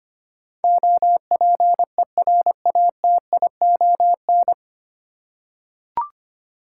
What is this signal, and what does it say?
Morse code, heard as a keyed tone, about 25 wpm, 700 Hz. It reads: OPERATION
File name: Nine Letter Words - 25wpm